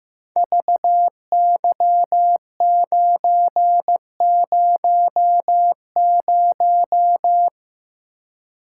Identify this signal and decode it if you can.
Morse code, heard as a keyed tone, about 15 wpm, 700 Hz. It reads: VY900
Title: Code Groups - 5 Letter:Numbers - 15wpm